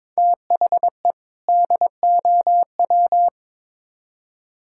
Morse code, heard as a keyed tone, about 22 wpm, 700 Hz. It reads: THE DOW
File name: practice / 22wpm / Build A Phrase - 5-Word Phrases - 22wpm